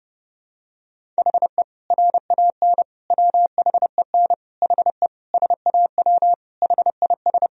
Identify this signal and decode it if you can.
Morse code, heard as a keyed tone, about 30 wpm, 700 Hz. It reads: HE RAN WHEN HE SAW HIS
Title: Sentences from Top 500 Words and Abbreviations 30wpm